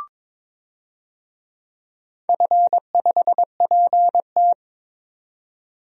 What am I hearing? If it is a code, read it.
Morse code, heard as a keyed tone, about 22 wpm, 700 Hz. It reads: F5PT
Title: DXCC-100 Call Signs - 4 Characters - 22wpm